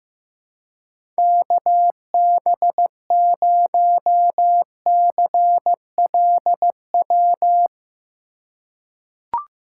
Morse code, heard as a keyed tone, about 15 wpm, 700 Hz. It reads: KB0CLW
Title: USA Call Signs - 2x Character Spacing - 15wpm